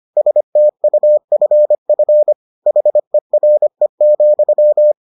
Morse code, heard as a keyed tone, about 25 wpm, 600 Hz. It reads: STUFF HERE,